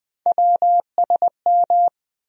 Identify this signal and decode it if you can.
Morse code, heard as a keyed tone, about 20 wpm, 700 Hz. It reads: WSM
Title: USA Call Sign Postfixes - 3 Letters 20wpm